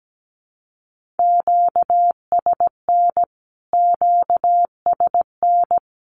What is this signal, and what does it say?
Morse code, heard as a keyed tone, about 17 wpm, 700 Hz. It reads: QSN QSN
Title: Q-Codes Repeated 3x 17wpm